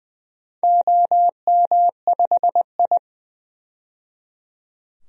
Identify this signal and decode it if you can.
Morse code, heard as a keyed tone, about 20 wpm, 700 Hz. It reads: OM5I
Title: DXCC-100 Call Signs - 4 Characters - 20wpm